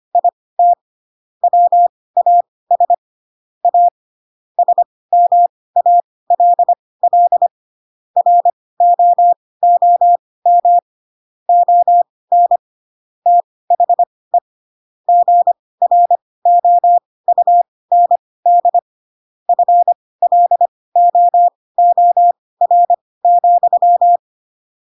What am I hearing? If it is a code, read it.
Morse code, heard as a keyed tone, about 25 wpm, spaced as for 18 wpm, 700 Hz. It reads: IT WAS A SMALL ROOM ON THE GROUND FLOOR,
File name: Pinocchio ch3 12 25 ik1hge